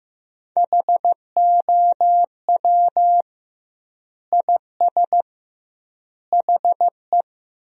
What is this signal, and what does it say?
Morse code, heard as a keyed tone, about 15 wpm, 700 Hz. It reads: HOW IS HE
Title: 2x Word-Spacing - Sentences from Top 100 Words - 15wpm